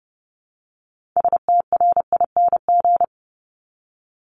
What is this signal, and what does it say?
Morse code, heard as a keyed tone, about 30 wpm, 700 Hz. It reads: STRING
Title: Multi-letter Phonemes Repeated 3x 30wpm